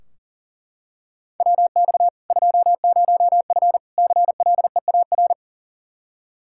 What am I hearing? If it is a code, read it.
Morse code, heard as a keyed tone, about 38 wpm, 700 Hz. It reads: WX 20F CLEAR
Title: K1USN SST QSO Practice 40wpm